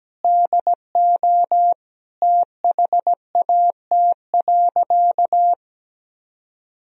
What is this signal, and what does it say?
Morse code, heard as a keyed tone, about 17 wpm, 700 Hz. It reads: DO THAT.
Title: Sentences from Top 1000 Words - Review Part 4 - 17wpm